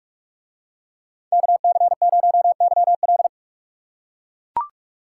Morse code, heard as a keyed tone, about 44 wpm, 700 Hz. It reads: KC0YL